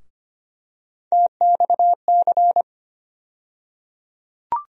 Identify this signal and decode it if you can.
Morse code, heard as a keyed tone, about 25 wpm, 700 Hz. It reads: TXC
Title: USA Call Sign Postfixes - 3 Letters 25wpm